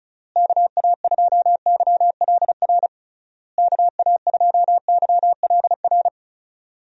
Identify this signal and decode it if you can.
Morse code, heard as a keyed tone, about 36 wpm, 700 Hz. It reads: KA2YLR KA2YLR
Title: USA Call Signs - 6 Characters - Repeated 3x - 35wpm